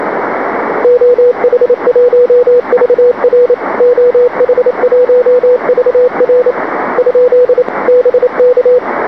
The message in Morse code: OH1VR OH1VR ? BK